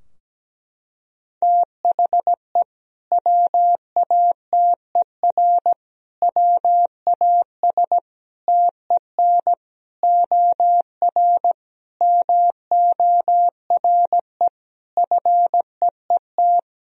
Morse code, heard as a keyed tone, about 17 wpm, 700 Hz. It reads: THE WATER WAS TEN OR MORE FEET
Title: Sentences from Top 1000 Words - Review Part 3 - 17wpm